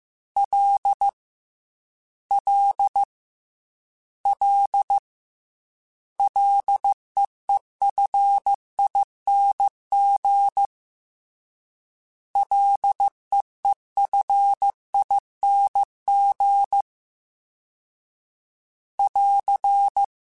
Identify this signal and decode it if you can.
Morse code, heard as a keyed tone, about 15 wpm, 800 Hz. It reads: L L L LEEFING LEEFING <AR>
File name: Ls / Single L m3s / Single L @13